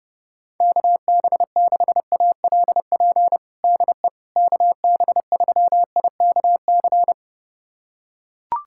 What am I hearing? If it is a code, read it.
Morse code, heard as a keyed tone, about 30 wpm, 700 Hz. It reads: KB6ALP DE KB3IKC
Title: USA Callsign DE Callsign 30wpm